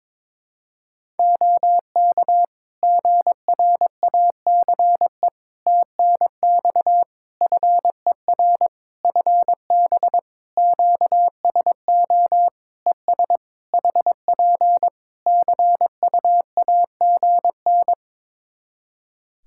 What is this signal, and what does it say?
Morse code, heard as a keyed tone, about 22 wpm, 700 Hz. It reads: OK GRACE TNX FER FB QSO ES HP CUAGN
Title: Practice QSO - Exchange 4 - Ending - 22wpm